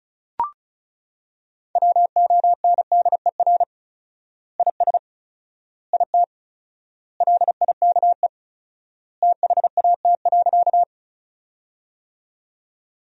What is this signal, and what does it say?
Morse code, heard as a keyed tone, about 36 wpm, 700 Hz. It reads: WONDER IS IT LIKE THAT.
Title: 4x Word-Spacing - Sentences from Top 500 Words - 35wpm